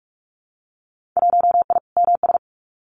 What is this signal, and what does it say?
Morse code, heard as a keyed tone, about 46 wpm, 700 Hz. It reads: 1I MS
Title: Winter Field Day Exchanges Encore 45wpm